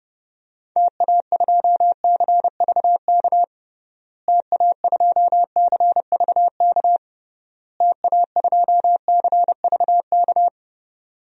Morse code, heard as a keyed tone, about 30 wpm, 700 Hz. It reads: TA2CVK TA2CVK TA2CVK
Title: DXCC-100 Call Signs - 6 Characters - Repeated 3x - 30wpm